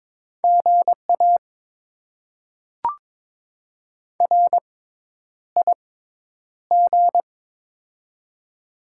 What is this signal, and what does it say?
Morse code, heard as a keyed tone, about 22 wpm, spaced as for 10 wpm, 700 Hz. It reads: GA RIG
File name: Top 20 QSO Elements - 6x Character Spacing - 22wpm